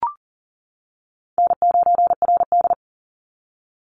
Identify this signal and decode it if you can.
Morse code, heard as a keyed tone, about 40 wpm, 700 Hz. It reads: N9RD
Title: Easy Call Signs - 40wpm